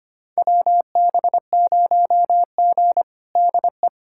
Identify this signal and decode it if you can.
Morse code, heard as a keyed tone, about 25 wpm, 700 Hz. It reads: WB0G DE
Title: USA Callsign DE Callsign - 4 Characters - 25wpm